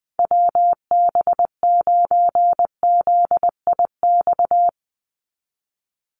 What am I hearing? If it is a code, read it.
Morse code, heard as a keyed tone, about 20 wpm, 700 Hz. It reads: WB9ZIX
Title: USA Call Signs 20wpm